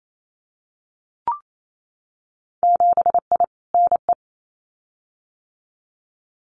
Morse code, heard as a keyed tone, about 28 wpm, 700 Hz. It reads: 7I NE
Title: Winter Field Day Exchanges Encore 28wpm